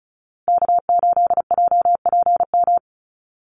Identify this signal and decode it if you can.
Morse code, heard as a keyed tone, about 34 wpm, 700 Hz. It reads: K8JPM
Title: Call Signs Repeated 2x - 35wpm